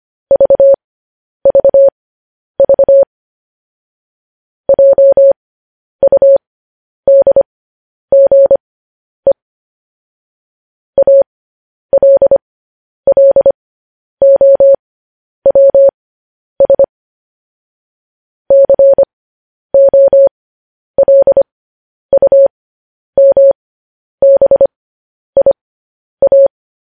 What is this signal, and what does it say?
Morse code, heard as a keyed tone, about 25 wpm, spaced as for 10 wpm, 550 Hz. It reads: VVV JUDGE ALLOWS COLUMBIA
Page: news headlines in morse code at 10 wpm